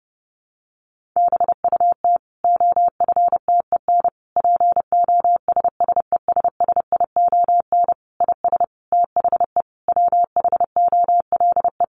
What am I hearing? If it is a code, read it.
Morse code, heard as a keyed tone, about 30 wpm, 700 Hz. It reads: BUT OFTEN POSSESSION IS THE WHOLE
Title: Sentences from Top 1400 Words 30wpm